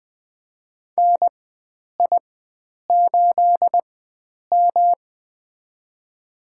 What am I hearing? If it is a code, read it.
Morse code, heard as a keyed tone, about 20 wpm, spaced as for 9 wpm, 700 Hz. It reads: NI8M
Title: USA Call Signs - 4 Characters - 4x Character Spacing - 20wpm